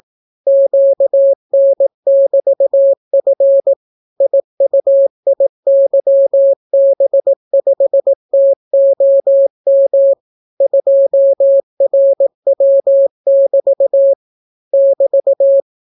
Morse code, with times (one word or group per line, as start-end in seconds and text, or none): 0.5-3.7 QN<BT>F
4.2-10.1 IUIYB5TOM
10.6-14.1 2RW<BT>
14.7-15.6 <BT>